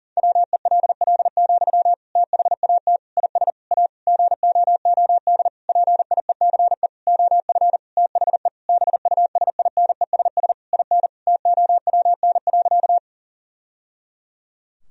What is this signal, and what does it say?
Morse code, heard as a keyed tone, about 40 wpm, 700 Hz. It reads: WELL, THAT IS A GOOD PIECE OF THE BUSINESS IN TOWN.
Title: Sentences from Top 1000 Words - Review Part 2 - 40wpm